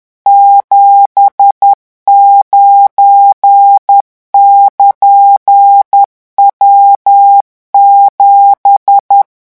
Morse code, heard as a keyed tone, about 10 wpm, 800 Hz. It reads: 79<KN>W7
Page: REVISION 18 - 10 mots minute